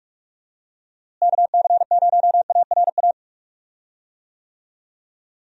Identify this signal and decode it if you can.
Morse code, heard as a keyed tone, about 41 wpm, 700 Hz. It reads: KC0ARA
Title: USA Call Signs - 6 Characters - Repeated 2x - 45wpm